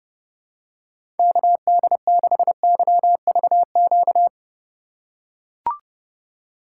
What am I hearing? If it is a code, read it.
Morse code, heard as a keyed tone, about 30 wpm, 700 Hz. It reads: KD6YVQ
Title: USA Call Signs - 2x Character Spacing - 30wpm